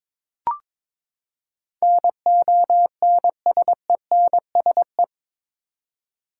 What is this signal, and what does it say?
Morse code, heard as a keyed tone, about 22 wpm, 700 Hz. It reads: NONSENSE
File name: practice / 22wpm / Eight Letter Words - 22wpm